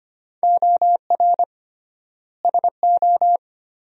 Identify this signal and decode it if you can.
Morse code, heard as a keyed tone, about 25 wpm, 700 Hz. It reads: OR SO
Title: 3x Word-Spacing - Sentences from Top 100 Words - 25wpm